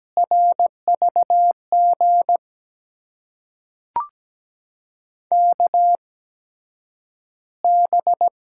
Morse code, heard as a keyed tone, about 17 wpm, spaced as for 9 wpm, 700 Hz. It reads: RVG KB
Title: USA Call Signs - 6 Characters - 8x Character Spacing - 17wpm